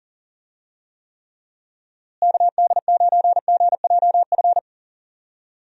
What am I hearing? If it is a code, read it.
Morse code, heard as a keyed tone, about 40 wpm, 700 Hz. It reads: KD9GJF
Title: USA Call Signs - 6 Characters - Repeated 2x - 40wpm